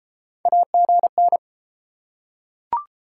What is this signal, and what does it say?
Morse code, heard as a keyed tone, about 31 wpm, 700 Hz. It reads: AGN